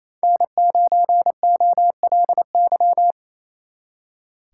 Morse code, heard as a keyed tone, about 36 wpm, 700 Hz. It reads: N9OLY